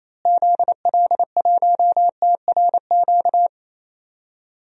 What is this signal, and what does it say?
Morse code, heard as a keyed tone, about 28 wpm, 700 Hz. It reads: ZL1TRQ
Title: DXCC-100 Call Signs - 6 Characters - Repeated 2x - 28wpm